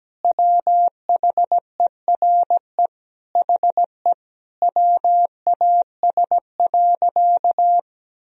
Morse code, heard as a keyed tone, about 17 wpm, 700 Hz. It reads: WHERE HE WAS.
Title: Sentences from Top 1000 Words - Review Part 1 - 17wpm